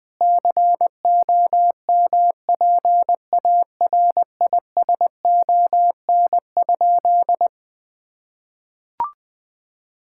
Morse code, heard as a keyed tone, about 20 wpm, 700 Hz. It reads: COMPARISON?
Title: Sentences from Top 1600 Words 20wpm